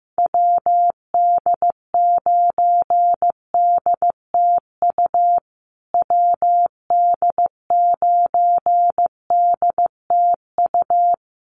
Morse code, heard as a keyed tone, about 15 wpm, 700 Hz. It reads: WD9DTU WD9DTU
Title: USA Call Signs Repeated 3x 15wpm